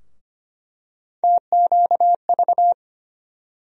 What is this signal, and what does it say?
Morse code, heard as a keyed tone, about 25 wpm, 700 Hz. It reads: TQV
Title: USA Call Sign Postfixes - 3 Letters 25wpm